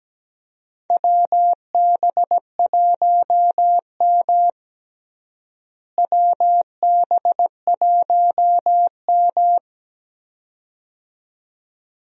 Morse code, heard as a keyed tone, about 17 wpm, 700 Hz. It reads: WB1M WB1M
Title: USA Call Signs - 4 Characters - Repeated 2x - 17wpm